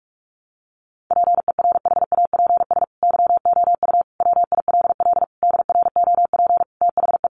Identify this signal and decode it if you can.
Morse code, heard as a keyed tone, about 44 wpm, 700 Hz. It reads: PERHAPS YOU WILL DROP THE
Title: Sentences from Top 1400 Words 45wpm